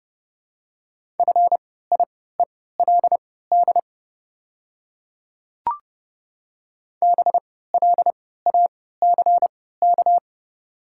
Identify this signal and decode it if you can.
Morse code, heard as a keyed tone, about 30 wpm, spaced as for 17 wpm, 700 Hz. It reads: FIELD BLACK
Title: 3x Character Spacing - Top 500 Words - 30wpm